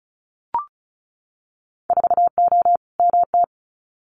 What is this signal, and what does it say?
Morse code, heard as a keyed tone, about 36 wpm, 700 Hz. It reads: 4O MT
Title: Winter Field Day Exchanges Encore 35wpm